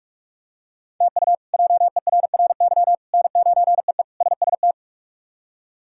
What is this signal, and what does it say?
Morse code, heard as a keyed tone, about 46 wpm, 700 Hz. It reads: TU JERRY N9EE SST
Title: K1USN SST QSO Practice - 45wpm